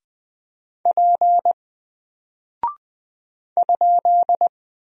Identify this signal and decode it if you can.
Morse code, heard as a keyed tone, about 20 wpm, 700 Hz. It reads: P ?
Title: Single Character (Letter:Number:Punctuation:Prosign) Alt 20wpm